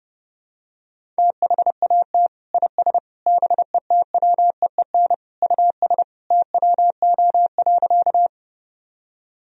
Morse code, heard as a keyed tone, about 30 wpm, 700 Hz. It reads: THAT IS BETWEEN US TWO.
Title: Sentences from Top 1000 Words - Review Part 1 - 30wpm